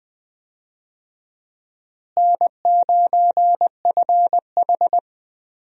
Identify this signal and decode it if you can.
Morse code, heard as a keyed tone, about 20 wpm, 700 Hz. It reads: N9FH
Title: USA Call Signs - 4 Characters - Repeated 3x - 20wpm